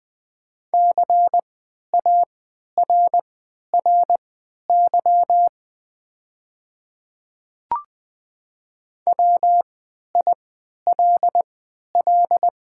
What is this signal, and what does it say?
Morse code, heard as a keyed tone, about 20 wpm, spaced as for 11 wpm, 690 Hz. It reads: CARRY WILL
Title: 3x Character Spacing - Top 300 Words - 20wpm